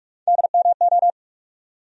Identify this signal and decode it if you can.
Morse code, heard as a keyed tone, about 41 wpm, 700 Hz. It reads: DMO
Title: USA Call Sign Postfixes - 3 Letters 45wpm